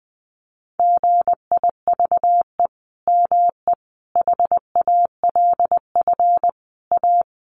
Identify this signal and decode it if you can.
Morse code, heard as a keyed tone, about 19 wpm, 700 Hz. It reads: GIVE ME HALF A
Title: Sentences from Top 1500 Words 20wpm